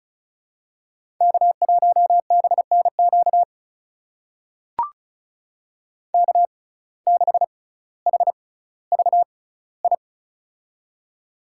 Morse code, heard as a keyed tone, about 36 wpm, spaced as for 18 wpm, 700 Hz. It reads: K1BNQ K6HVI
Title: USA Call Signs - 5 Characters - 6x Character Spacing - 35wpm